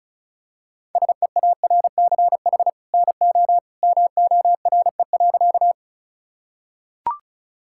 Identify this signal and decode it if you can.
Morse code, heard as a keyed tone, about 36 wpm, 700 Hz. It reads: SEARCH NO MORE.
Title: Sentences from Top 1000 Words - Review Part 5 - 35wpm